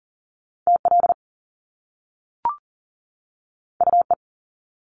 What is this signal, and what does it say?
Morse code, heard as a keyed tone, about 40 wpm, 700 Hz. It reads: TL UE
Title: Top Two Letter Combinations in Words 40wpm